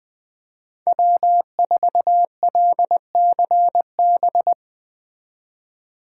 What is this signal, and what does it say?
Morse code, heard as a keyed tone, about 20 wpm, 700 Hz. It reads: W4LCB